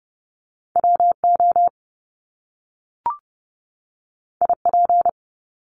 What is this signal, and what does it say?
Morse code, heard as a keyed tone, about 30 wpm, 700 Hz. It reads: WO IP
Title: Top Two Letter Combinations in Words 30wpm